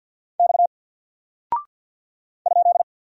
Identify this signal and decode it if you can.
Morse code, heard as a keyed tone, about 48 wpm, 700 Hz. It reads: X ?
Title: Single Character (Letter:Number:Punctuation:Prosign) Alt 50wpm